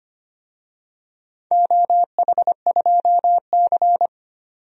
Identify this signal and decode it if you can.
Morse code, heard as a keyed tone, about 25 wpm, 700 Hz. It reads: OH2C
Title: DXCC-100 Call Signs - 4 Characters - 25wpm